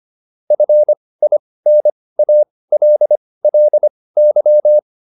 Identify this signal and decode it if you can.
Morse code, heard as a keyed tone, about 25 wpm, spaced as for 18 wpm, 600 Hz. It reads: FINALLY